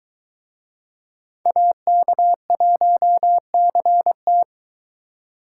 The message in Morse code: AK1CT

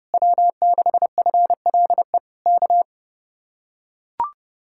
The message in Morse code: W6FLE K

